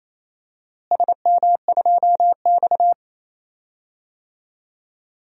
Morse code, heard as a keyed tone, about 28 wpm, 700 Hz. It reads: SM2X
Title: DXCC-100 Call Signs - 4 Characters - Repeated 3x - 28wpm